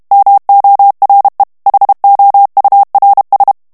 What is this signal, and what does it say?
Morse code, heard as a keyed tone, about 32 wpm, 800 Hz. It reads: MORE HOURS
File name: morse mp3 / ANGLAIS texte clair 1 vitesse 30 mots minute